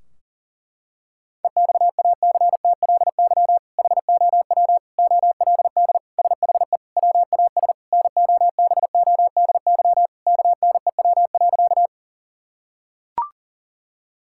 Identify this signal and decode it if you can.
Morse code, heard as a keyed tone, about 40 wpm, 700 Hz. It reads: EXACTLY HOW OLD SHE WAS NOBODY KNEW.
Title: Sentences from Top 1600 Words 40wpm